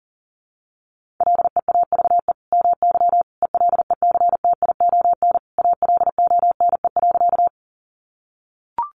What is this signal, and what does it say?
Morse code, heard as a keyed tone, about 40 wpm, 700 Hz. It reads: LEAVE MY ELECTION ALONE.
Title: Sentences from Top 1500 Words 40wpm